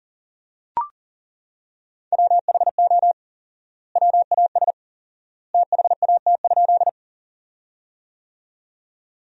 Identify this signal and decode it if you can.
Morse code, heard as a keyed tone, about 40 wpm, 700 Hz. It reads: WHO WAS THAT?
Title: 4x Word-Spacing - Sentences from Top 100 Words - 40wpm